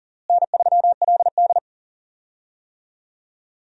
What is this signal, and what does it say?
Morse code, heard as a keyed tone, about 37 wpm, 700 Hz. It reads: N3LD
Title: USA Call Signs - 4 Characters - Repeated 2x - 40wpm